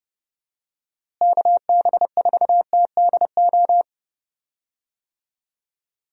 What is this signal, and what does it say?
Morse code, heard as a keyed tone, about 30 wpm, 700 Hz. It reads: KB4TDO
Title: USA Call Signs - 6 Characters - Repeated 3x - 30wpm